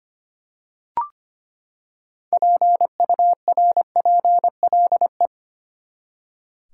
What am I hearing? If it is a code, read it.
Morse code, heard as a keyed tone, about 25 wpm, 700 Hz. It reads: PURPLE
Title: Six Letter Words - 25wpm